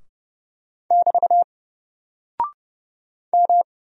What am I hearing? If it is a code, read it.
Morse code, heard as a keyed tone, about 30 wpm, 700 Hz. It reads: <BT> M
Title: Single Character (Letter:Number:Punctuation:Prosign) Alt 30wpm